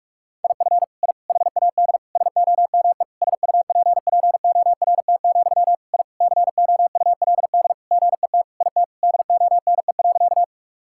Morse code, heard as a keyed tone, about 46 wpm, 700 Hz. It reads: IF I HAD SOME SUPPORT, I COULD GET IT DONE.
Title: Sentences from Top 1000 Words - Review Part 5 - 45wpm